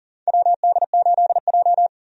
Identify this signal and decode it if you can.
Morse code, heard as a keyed tone, about 40 wpm, 700 Hz. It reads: WD8J
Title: USA Call Signs - 4 Characters - Repeated 3x - 40wpm